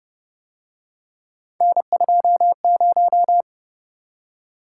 Morse code, heard as a keyed tone, about 30 wpm, 700 Hz. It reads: N20